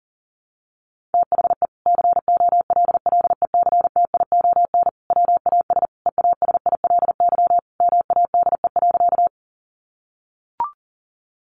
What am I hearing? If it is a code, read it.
Morse code, heard as a keyed tone, about 40 wpm, 700 Hz. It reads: THE COLLECTION WAS EASILY MADE.
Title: Sentences from Top 1200 Words 40wpm